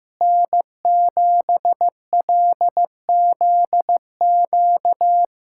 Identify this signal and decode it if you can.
Morse code, heard as a keyed tone, about 15 wpm, 700 Hz. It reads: N7LZQ
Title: USA Callsign DE Callsign - 5 Characters - 15wpm